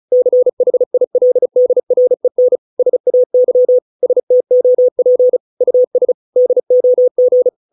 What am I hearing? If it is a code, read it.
Morse code, heard as a keyed tone, about 34 wpm, 500 Hz. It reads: CHILDREN SAY STOP US DOG